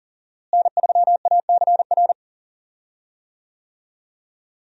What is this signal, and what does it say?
Morse code, heard as a keyed tone, about 40 wpm, 700 Hz. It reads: N3ACR
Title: USA Call Signs - 5 Characters - Repeated 3x - 40wpm